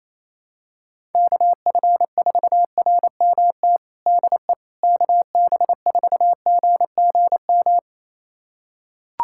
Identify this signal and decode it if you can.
Morse code, heard as a keyed tone, about 28 wpm, 700 Hz. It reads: KF4RMT DE KB4GGM
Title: USA Callsign DE Callsign - 6 Characters - 28wpm